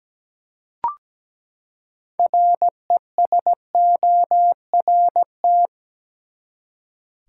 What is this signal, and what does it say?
Morse code, heard as a keyed tone, about 17 wpm, 700 Hz. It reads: RESORT
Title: Six Letter Words - 17wpm